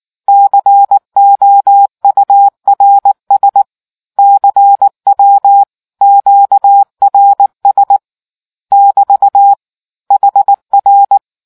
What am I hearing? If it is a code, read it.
Morse code, heard as a keyed tone, about 19 wpm, 800 Hz. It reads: COURS CW QRS <BT> HR